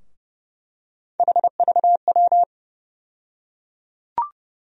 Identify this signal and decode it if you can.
Morse code, heard as a keyed tone, about 30 wpm, 700 Hz. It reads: HVW
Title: USA Call Sign Postfixes - 3 Letters 30wpm